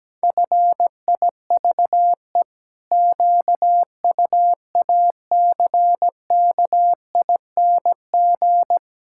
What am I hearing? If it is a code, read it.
Morse code, heard as a keyed tone, about 17 wpm, 700 Hz. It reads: FIVE QUACKING
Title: Pangram Challenge v2 - 17wpm